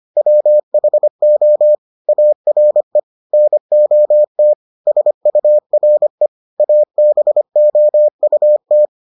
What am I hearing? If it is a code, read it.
Morse code, heard as a keyed tone, about 25 wpm, 600 Hz. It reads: WHO ARE NOT SURE ABOUT